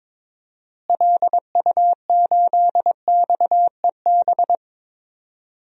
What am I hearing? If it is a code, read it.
Morse code, heard as a keyed tone, about 22 wpm, 700 Hz. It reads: LU8XEB
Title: DXCC-100 Call Signs - 6 Characters - Repeated 2x - 22wpm